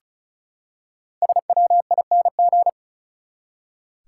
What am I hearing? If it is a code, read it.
Morse code, heard as a keyed tone, about 36 wpm, 700 Hz. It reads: SWING
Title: Five Letter Words - 35wpm